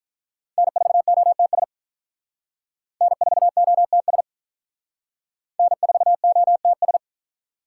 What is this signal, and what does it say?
Morse code, heard as a keyed tone, about 43 wpm, 700 Hz. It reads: N4OTS N4OTS N4OTS
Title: Speed Racing - USA 1x3 Call Signs 35wpm